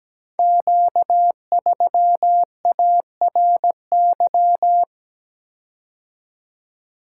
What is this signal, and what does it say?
Morse code, heard as a keyed tone, about 17 wpm, 700 Hz. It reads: Q3ARY
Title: Code Groups - 5 Letter:Numbers - 17wpm